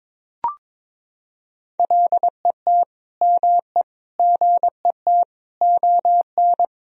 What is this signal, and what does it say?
Morse code, heard as a keyed tone, about 22 wpm, 700 Hz. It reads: LET ME GET ON
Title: Sentences from Top 300 Words and Abbreviations 22wpm